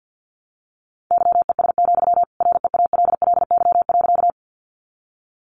Morse code, heard as a keyed tone, about 50 wpm, 700 Hz. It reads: YES, REALLY.
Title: Sentences from Top 600 Words 50wpm